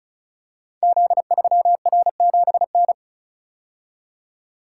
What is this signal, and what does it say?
Morse code, heard as a keyed tone, about 36 wpm, 700 Hz. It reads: Z3R7N
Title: Code Groups - 5 Letter:Numbers - 35wpm